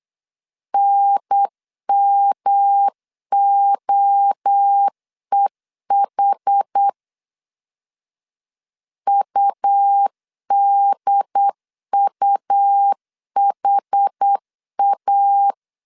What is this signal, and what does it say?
Morse code, heard as a keyed tone, about 8 wpm, 800 Hz. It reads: NMOEH UDUHA